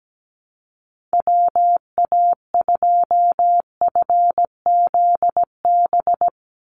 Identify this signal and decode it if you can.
Morse code, heard as a keyed tone, about 17 wpm, 700 Hz. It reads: WA2FZB
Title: Call Signs - 17wpm